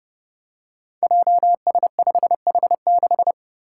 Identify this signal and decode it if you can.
Morse code, heard as a keyed tone, about 30 wpm, 700 Hz. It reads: JS5H6
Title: Five Challenging Letter-Number Combinations 30wpm